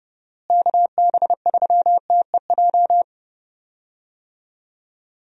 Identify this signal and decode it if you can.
Morse code, heard as a keyed tone, about 30 wpm, 700 Hz. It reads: KB3TEJ
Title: USA Call Signs - 6 Characters - Repeated 2x - 30wpm